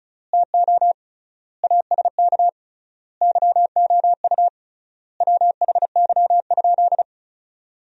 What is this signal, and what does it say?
Morse code, heard as a keyed tone, about 34 wpm, 700 Hz. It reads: TO ASK YOU WHY?
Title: 3x Word-Spacing - Sentences from Top 200 Words - 35wpm